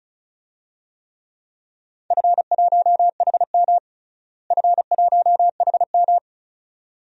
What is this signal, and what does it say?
Morse code, heard as a keyed tone, about 34 wpm, 700 Hz. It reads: F1HM F1HM
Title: DXCC-100 Call Signs - 4 Characters - Repeated 2x - 35wpm